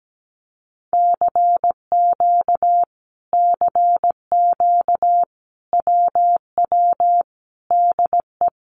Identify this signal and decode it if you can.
Morse code, heard as a keyed tone, about 17 wpm, 700 Hz. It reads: CQ CQ WW DE
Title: Common QSO Phrases Encore 17wpm